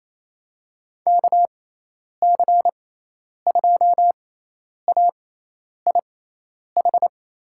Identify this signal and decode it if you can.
Morse code, heard as a keyed tone, about 28 wpm, spaced as for 10 wpm, 700 Hz. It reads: KC2AIH
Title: USA Call Signs - 6 Characters - 6x Character Spacing - 28wpm